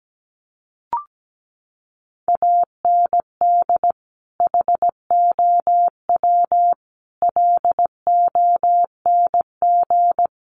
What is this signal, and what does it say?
Morse code, heard as a keyed tone, about 17 wpm, 700 Hz. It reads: AND HOW LONG
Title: Sentences from Top 1200 Words 17wpm